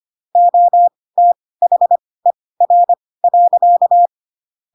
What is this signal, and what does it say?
Morse code, heard as a keyed tone, about 25 wpm, spaced as for 18 wpm, 700 Hz. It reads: OTHER.
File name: Pinocchio / Pinocchio ch1 12 25 ik1hge